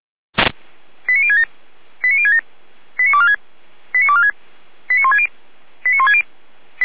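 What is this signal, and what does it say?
Five-tone selective call (ZVEI), heard as quick runs of five tones, tones of about 70 ms. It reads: 89067 89067 89246 89246 89160 89160
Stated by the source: Five-tone ZVEI selective call received on an SDR.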